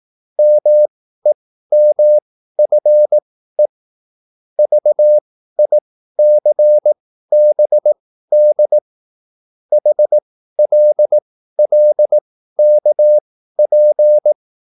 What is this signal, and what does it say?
Morse code, heard as a keyed tone, about 18 wpm, spaced as for 13 wpm, 600 Hz. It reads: MEMFE VICBD HLLKP